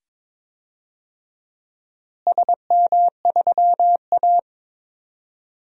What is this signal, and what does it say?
Morse code, heard as a keyed tone, about 22 wpm, 700 Hz. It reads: SM3A
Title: DXCC-100 Call Signs - 4 Characters - Repeated 2x - 22wpm